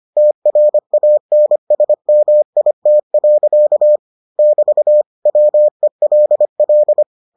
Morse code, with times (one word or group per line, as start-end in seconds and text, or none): 0.2-4.0 TRANSMIT.
4.4-5.0 <BT>
5.3-7.0 WELL